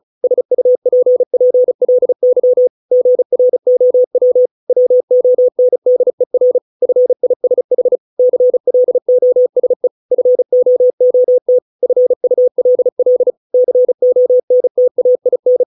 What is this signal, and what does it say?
Morse code, heard as a keyed tone, about 35 wpm, 500 Hz. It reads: SUPPLY GROW WONDER FISH CLOSE FOOT FULL CONTAIN